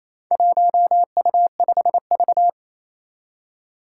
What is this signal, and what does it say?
Morse code, heard as a keyed tone, about 28 wpm, 700 Hz. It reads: 1U5V
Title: Four Challenging Letter-Number Combinations 28wpm